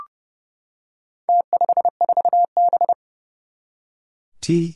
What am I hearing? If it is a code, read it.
Morse code, heard as a keyed tone, about 31 wpm, 700 Hz. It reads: T54B
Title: Four Challenging Letter-Number Combinations 30wpm